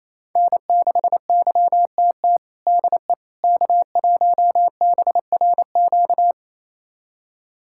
Morse code, heard as a keyed tone, about 28 wpm, 700 Hz. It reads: N6YTT DE K1BRQ
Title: USA Callsign DE Callsign 28wpm